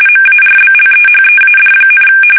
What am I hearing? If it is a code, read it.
RTTY, about 45.45 baud. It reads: MARBLEHEAD,
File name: fsk850 / usnavy1